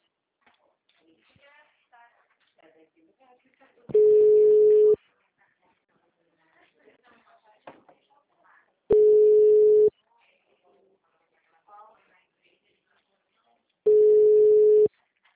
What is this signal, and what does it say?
Ringback tone (Europe, 425 Hz), with speech.